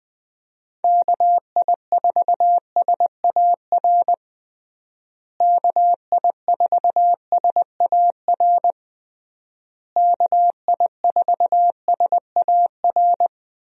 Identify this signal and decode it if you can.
Morse code, heard as a keyed tone, about 21 wpm, 700 Hz. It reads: KI4SAR KI4SAR KI4SAR
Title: USA Call Signs - 6 Characters - Repeated 3x - 20wpm